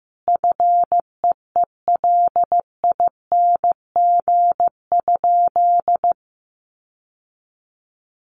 Morse code, heard as a keyed tone, about 15 wpm, 700 Hz. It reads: FEELING?
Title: Sentences from Top 1200 Words 15wpm